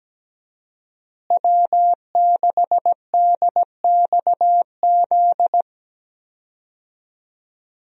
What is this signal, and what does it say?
Morse code, heard as a keyed tone, about 17 wpm, 700 Hz. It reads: W6DXZ
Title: USA Call Signs - 5 Characters - Repeated 2x - 17wpm